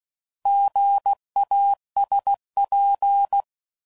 Morse code, heard as a keyed tone, about 16 wpm, 800 Hz. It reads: GASP
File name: Prefixes / By First Letter / Gs